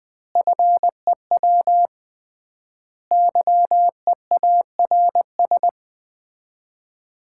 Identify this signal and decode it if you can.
Morse code, heard as a keyed tone, about 21 wpm, 700 Hz. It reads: FEW YEARS
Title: 3x Word-Spacing - Sets of 5 Words - 20wpm